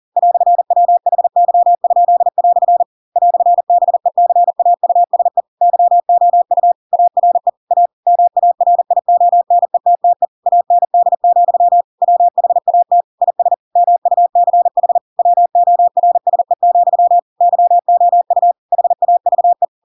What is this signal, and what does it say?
Morse code, heard as a keyed tone, about 40 wpm, 700 Hz. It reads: "WHY?" "BECAUSE YOU ARE A MARIONETTE AND, WHAT IS MUCH WORSE, YOU HAVE
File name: Pinocchio / Pinocchio ch4 40 40 ik1hge